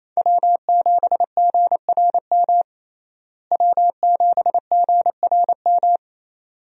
Morse code, heard as a keyed tone, about 27 wpm, 700 Hz. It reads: W7GRM W7GRM
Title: USA Call Signs - 5 Characters - Repeated 2x - 28wpm